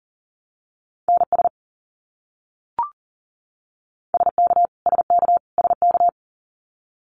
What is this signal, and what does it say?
Morse code, heard as a keyed tone, about 40 wpm, 700 Hz. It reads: NS SK SK SK
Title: Canadian Provinces and Territories - Repeated 3x - 40wpm